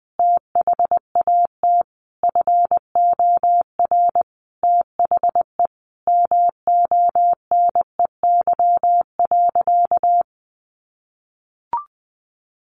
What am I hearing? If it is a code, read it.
Morse code, heard as a keyed tone, about 20 wpm, 700 Hz. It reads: THAT FOR THE MONEY.
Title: Sentences from Top 600 Words 20wpm